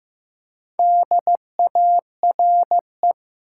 Morse code, heard as a keyed tone, about 16 wpm, 700 Hz. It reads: DARE
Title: Four Letter Words - 15wpm